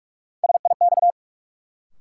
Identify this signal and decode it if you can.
Morse code, heard as a keyed tone, about 41 wpm, 700 Hz. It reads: SIX
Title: Sentences from Top 600 Words - New Word Drill - 45wpm